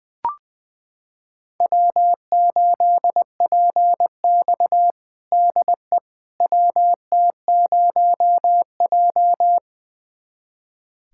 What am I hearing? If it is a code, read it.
Morse code, heard as a keyed tone, about 20 wpm, 700 Hz. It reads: W8PX DE WT0J
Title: USA Callsign DE Callsign - 4 Characters - 20wpm